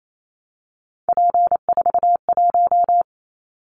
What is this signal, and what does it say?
Morse code, heard as a keyed tone, about 28 wpm, 700 Hz. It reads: P41
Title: Callsign Prefixes for Top 100 DXCC - 28wpm